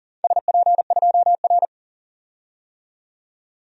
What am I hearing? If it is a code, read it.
Morse code, heard as a keyed tone, about 42 wpm, 700 Hz. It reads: SP2R
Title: DXCC-100 Call Signs - 4 Characters - 40wpm